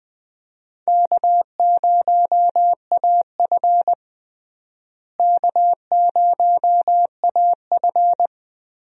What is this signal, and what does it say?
Morse code, heard as a keyed tone, about 20 wpm, 700 Hz. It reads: K0AF K0AF
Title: USA Call Signs - 4 Characters - Repeated 2x - 20wpm